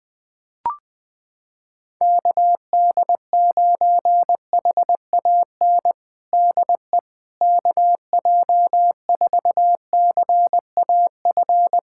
Morse code, heard as a keyed tone, about 20 wpm, 700 Hz. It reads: KD9HAN DE KJ4CAF
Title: USA Callsign DE Callsign 20wpm